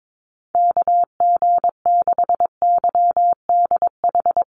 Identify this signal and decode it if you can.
Morse code, heard as a keyed tone, about 22 wpm, 700 Hz. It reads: KG6YDH
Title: USA Call Signs Repeated 3x 22wpm